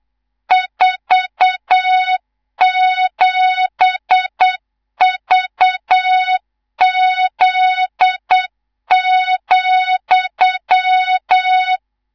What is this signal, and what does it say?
Morse code, heard as a keyed tone, about 8 wpm, 3700 Hz, with noise. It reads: 47VZ,
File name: Morze-HA2MN / hang / 19 GYAKORLAT